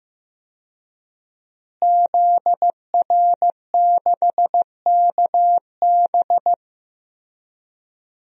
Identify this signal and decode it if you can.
Morse code, heard as a keyed tone, about 16 wpm, 700 Hz. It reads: ZR6KB
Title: DXCC-100 Call Signs - 5 Characters - Repeated 2x - 15wpm